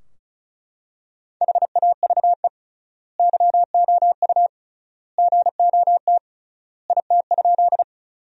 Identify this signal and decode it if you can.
Morse code, heard as a keyed tone, about 36 wpm, 700 Hz. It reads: HAVE YOU GOT IT?
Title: 3x Word-Spacing - Sentences from Top 300 Words - 35wpm